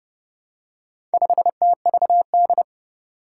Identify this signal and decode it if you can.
Morse code, heard as a keyed tone, about 30 wpm, 700 Hz. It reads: 5TVD
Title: Four Challenging Letter-Number Combinations 30wpm